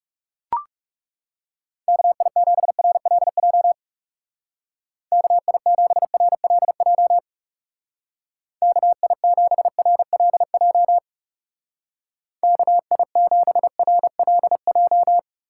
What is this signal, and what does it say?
Morse code, heard as a keyed tone, about 36 wpm, 700 Hz. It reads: KI7RLJ KI7RLJ KI7RLJ KI7RLJ
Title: Speed Racing - USA 2x3 Call Signs 30wpm